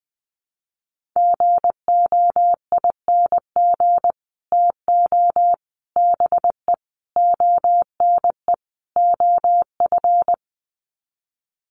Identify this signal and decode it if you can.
Morse code, heard as a keyed tone, about 20 wpm, 700 Hz. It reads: GOING TO BE ONE OF
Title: Sets of 5 Words Encore2 20wpm